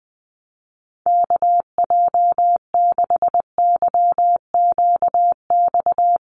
Morse code, heard as a keyed tone, about 19 wpm, 700 Hz. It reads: KJ6YQX
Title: USA Call Signs Repeated 3x 20wpm